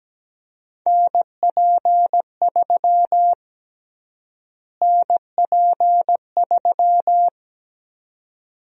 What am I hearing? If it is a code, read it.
Morse code, heard as a keyed tone, about 17 wpm, 700 Hz. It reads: NP3 NP3
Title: USA Call Sign Prefixes - Repeated 2x - 17wpm